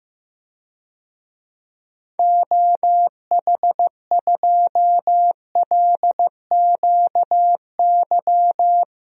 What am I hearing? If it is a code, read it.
Morse code, heard as a keyed tone, about 16 wpm, 700 Hz. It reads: OH2LQY
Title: DXCC-100 Call Signs - 6 Characters - 15wpm